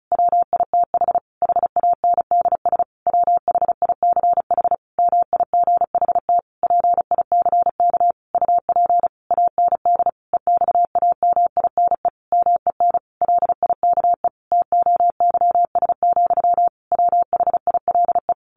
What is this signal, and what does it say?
Morse code, heard as a keyed tone, about 34 wpm, 700 Hz. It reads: WITH HANDS WHICH MIGHT PICK UP AND EXAMINE MEN LIKE TOYS, WHILE
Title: A Sound Of Thunder 35wpm